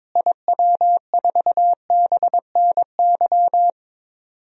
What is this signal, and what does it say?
Morse code, heard as a keyed tone, about 22 wpm, 700 Hz. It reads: IW4BNY
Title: DXCC-100 Call Signs - 6 Characters - Repeated 3x - 22wpm